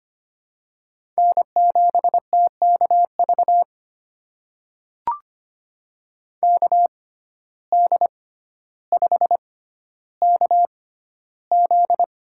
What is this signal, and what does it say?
Morse code, heard as a keyed tone, about 25 wpm, spaced as for 13 wpm, 700 Hz. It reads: N7TKV KD5KZ
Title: USA Call Signs - 5 Characters - 6x Character Spacing - 25wpm